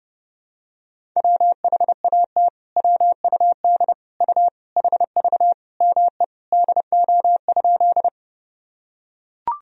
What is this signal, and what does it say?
Morse code, heard as a keyed tone, about 30 wpm, 700 Hz. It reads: WHAT WUD U HV ME DO?
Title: Sentences from Top 200 Words and Abbreviations 30wpm